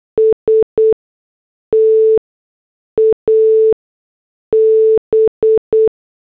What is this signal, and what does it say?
Morse code, heard as a keyed tone, about 8 wpm, spaced as for 6 wpm, 440 Hz. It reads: STAB
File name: morse / headlines